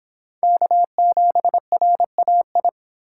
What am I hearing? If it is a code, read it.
Morse code, heard as a keyed tone, about 28 wpm, 700 Hz. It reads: K7RAI